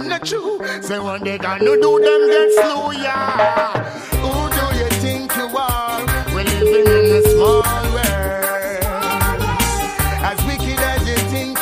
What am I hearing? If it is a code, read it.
Ringback tone (Europe, 425 Hz), with noise.